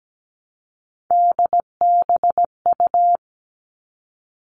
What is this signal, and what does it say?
Morse code, heard as a keyed tone, about 17 wpm, 700 Hz. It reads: DBU
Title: All Cut Numbers - 3-Digit 17wpm